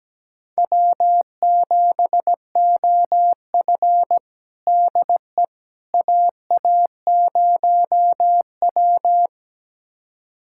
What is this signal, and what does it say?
Morse code, heard as a keyed tone, about 17 wpm, 700 Hz. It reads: W7OF DE AA0W
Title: USA Callsign DE Callsign - 4 Characters - 17wpm